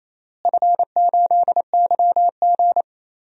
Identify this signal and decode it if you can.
Morse code, heard as a keyed tone, about 28 wpm, 700 Hz. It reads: F8YG
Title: DXCC-100 Call Signs - 4 Characters - Repeated 2x - 28wpm